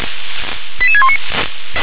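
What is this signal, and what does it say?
Five-tone selective call (ZVEI), heard as quick runs of five tones, tones of about 80 ms. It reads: 88619
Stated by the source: Five-tone ZVEI selective call received on an SDR.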